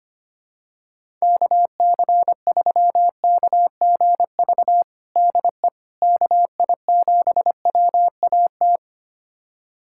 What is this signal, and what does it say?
Morse code, heard as a keyed tone, about 25 wpm, 700 Hz. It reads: KC3KGV DE KI7WAT
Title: USA Callsign DE Callsign 25wpm